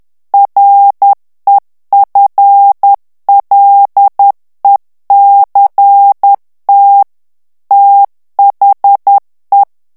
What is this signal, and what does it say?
Morse code, heard as a keyed tone, about 10 wpm, 800 Hz. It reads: REFLECT THE
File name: ANGLAIS texte clair 3 vitesse 10 mots minute